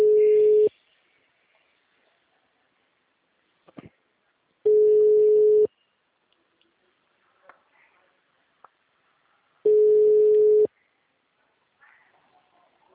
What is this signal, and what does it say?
Ringback tone (Europe, 425 Hz).
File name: out- -2009- - - .89098